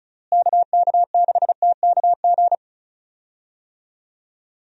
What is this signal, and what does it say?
Morse code, heard as a keyed tone, about 34 wpm, 700 Hz. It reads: KK6TKG
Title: USA Call Signs - 6 Characters - Repeated 3x - 35wpm